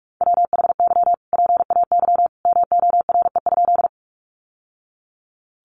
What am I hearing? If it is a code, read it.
Morse code, heard as a keyed tone, about 44 wpm, 700 Hz. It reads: WHY PAY MORE?
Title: Sentences from Top 700 Words 45wpm